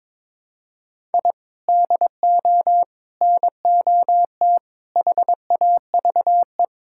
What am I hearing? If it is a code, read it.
Morse code, heard as a keyed tone, about 22 wpm, 700 Hz. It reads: I DO NOT HAVE
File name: practice / 22wpm / Build A Phrase - 5-Word Phrases - 22wpm